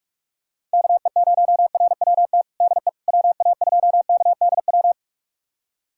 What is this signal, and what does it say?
Morse code, heard as a keyed tone, about 46 wpm, 700 Hz. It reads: KE0RWT DE WA2KDW
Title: USA Callsign DE Callsign - 6 Characters - 45wpm